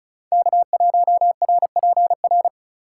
Morse code, heard as a keyed tone, about 42 wpm, 700 Hz. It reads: K1RPR
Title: Speed Racing - USA Call Signs 35wpm